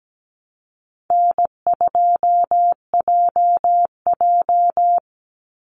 Morse code, heard as a keyed tone, about 17 wpm, 700 Hz. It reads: N2JJ
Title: Call Signs - 17wpm